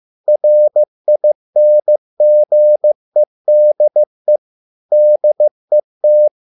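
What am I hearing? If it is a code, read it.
Morse code, heard as a keyed tone, about 15 wpm, 600 Hz. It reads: RINGEDE DET
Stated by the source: Morse code at a slow 15 wpm from CW med Gnister.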